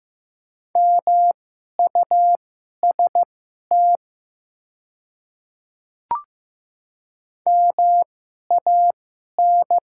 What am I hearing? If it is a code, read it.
Morse code, heard as a keyed tone, about 15 wpm, spaced as for 11 wpm, 700 Hz. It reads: MUST MAN
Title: 2x Character Spacing - Top 200 Words - 15wpm